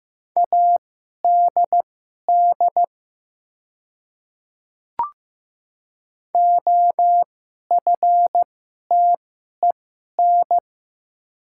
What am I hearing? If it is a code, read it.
Morse code, heard as a keyed tone, about 15 wpm, spaced as for 11 wpm, 700 Hz. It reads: ADD OFTEN
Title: 2x Character Spacing - Top 300 Words - 15wpm